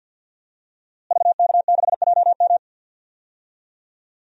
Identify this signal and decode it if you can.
Morse code, heard as a keyed tone, about 51 wpm, 700 Hz. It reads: VK6JM
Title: DXCC-100 Call Signs - 5 Characters - Repeated 2x - 50wpm